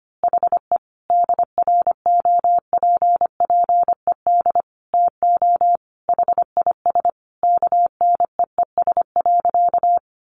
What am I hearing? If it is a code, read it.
Morse code, heard as a keyed tone, about 25 wpm, 700 Hz. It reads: HE DROPPED TO HIS KNEES.
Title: A Sound Of Thunder 25wpm